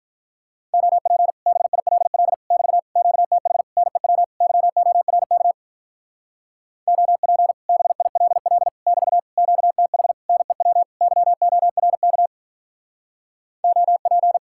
Morse code, heard as a keyed tone, about 46 wpm, 700 Hz. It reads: OP BILL <BT> QTH NEW YORK OP BILL <BT> QTH NEW YORK OP